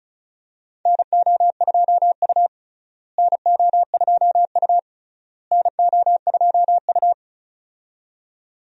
Morse code, heard as a keyed tone, about 36 wpm, 700 Hz. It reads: NO2U NO2U NO2U
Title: USA Call Signs - 4 Characters - Repeated 3x - 35wpm